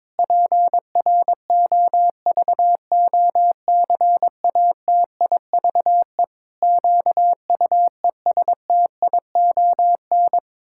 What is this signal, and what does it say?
Morse code, heard as a keyed tone, about 22 wpm, 700 Hz. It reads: PROVOCATIVE QUESTION
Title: Pangram Challenge v2 - 22wpm